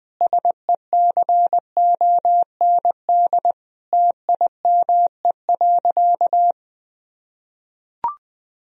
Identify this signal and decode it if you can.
Morse code, heard as a keyed tone, about 20 wpm, 700 Hz. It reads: SECOND TIME.
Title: Sentences from Top 1000 Words - Review Part 1 - 20wpm